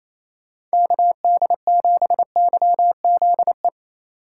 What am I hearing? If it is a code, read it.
Morse code, heard as a keyed tone, about 34 wpm, 700 Hz. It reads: KD7YZE
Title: Speed Racing - USA Call Signs 28wpm